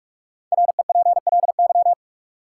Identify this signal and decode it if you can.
Morse code, heard as a keyed tone, about 45 wpm, 700 Hz. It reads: REPLY
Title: Five Letter Words - 45wpm